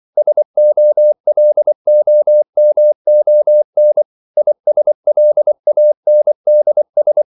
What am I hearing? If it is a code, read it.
Morse code, heard as a keyed tone, about 24 wpm, 600 Hz. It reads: SOLOMON ISLANDS
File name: CW / Practice / NationalCapitals24